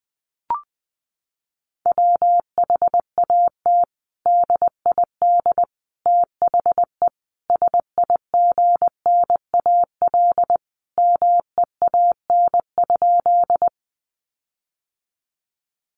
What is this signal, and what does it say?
Morse code, heard as a keyed tone, about 20 wpm, 700 Hz. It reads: WHAT DID THE SIGNAL MEAN?
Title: Sentences from Top 1400 Words 20wpm